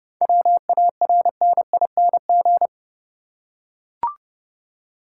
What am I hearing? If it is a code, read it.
Morse code, heard as a keyed tone, about 30 wpm, 700 Hz. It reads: WARNING
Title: Seven Letter Words - 30wpm